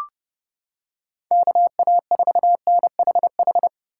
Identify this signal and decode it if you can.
Morse code, heard as a keyed tone, about 30 wpm, 700 Hz. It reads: KA4NHH